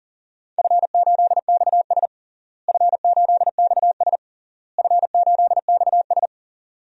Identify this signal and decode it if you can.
Morse code, heard as a keyed tone, about 40 wpm, 700 Hz. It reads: F8XS F8XS F8XS
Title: DXCC-100 Call Signs - 4 Characters - Repeated 3x - 40wpm